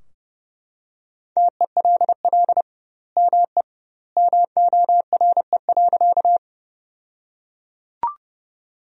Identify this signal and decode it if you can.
Morse code, heard as a keyed tone, about 30 wpm, 700 Hz. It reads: TELL ME MORE.
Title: 2x Word-Spacing - Sentences from Top 300 Words - 30wpm